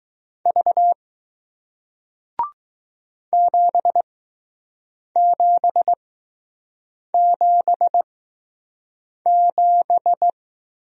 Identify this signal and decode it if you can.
Morse code, heard as a keyed tone, about 20 wpm, spaced as for 6 wpm, 700 Hz. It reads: V 7777